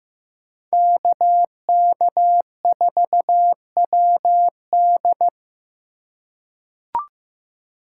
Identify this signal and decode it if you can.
Morse code, heard as a keyed tone, about 15 wpm, 700 Hz. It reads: KK4WD
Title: USA Call Signs - 5 Characters - 2x Character Spacing - 15wpm